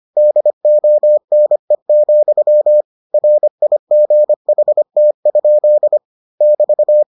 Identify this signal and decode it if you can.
Morse code, heard as a keyed tone, about 25 wpm, 600 Hz. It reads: DONE, RIGHT? <BT>